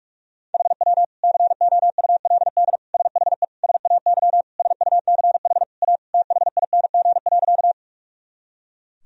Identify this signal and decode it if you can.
Morse code, heard as a keyed tone, about 46 wpm, 700 Hz. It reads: HW COULD SHE SAY SUCH A THING.
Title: Sentences from Top 200 Words and Abbreviations 45wpm